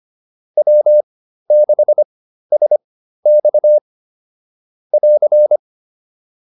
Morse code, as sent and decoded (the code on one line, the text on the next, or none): .-- -.... ... -..- / .-.-.
W6SX <AR>